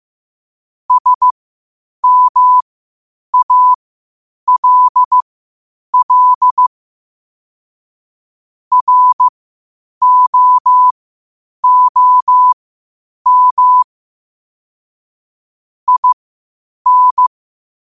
Morse code, heard as a keyed tone, about 15 wpm, spaced as for 8 wpm, 1000 Hz. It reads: SMALL ROOM IN
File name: qotdfiles / 8wpm